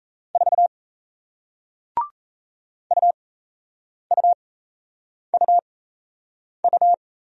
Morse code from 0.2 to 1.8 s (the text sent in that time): 4